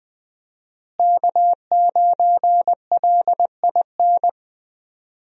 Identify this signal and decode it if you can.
Morse code, heard as a keyed tone, about 20 wpm, 700 Hz. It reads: K9LIN
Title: USA Call Signs - 5 Characters - Repeated 3x - 20wpm